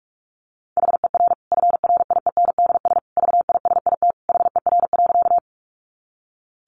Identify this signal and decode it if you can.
Morse code, heard as a keyed tone, about 44 wpm, 700 Hz. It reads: HER FRIENDS VISIT HER.
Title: Easy Phrases 45wpm